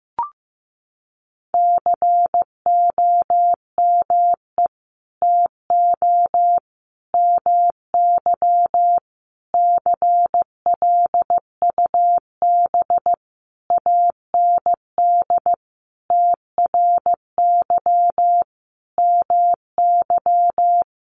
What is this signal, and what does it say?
Morse code, heard as a keyed tone, about 14 wpm, 700 Hz. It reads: COME TO MY CLUB AND TRY MY
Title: Sentences from Top 1500 Words 15wpm